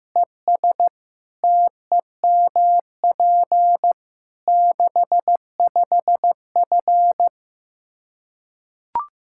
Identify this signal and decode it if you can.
Morse code, heard as a keyed tone, about 15 wpm, 700 Hz. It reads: ES TEMP 65F
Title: Practice QSO - Exchange 2 - Info - 15wpm